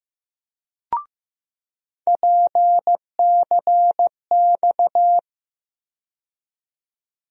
Morse code, heard as a keyed tone, about 15 wpm, 700 Hz. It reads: PCX
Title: USA Call Sign Postfixes - 3 Letters 15wpm